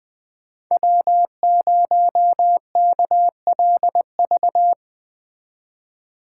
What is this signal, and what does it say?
Morse code, heard as a keyed tone, about 22 wpm, 700 Hz. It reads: W0KLV
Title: Speed Racing - USA 1x3 Call Signs 20wpm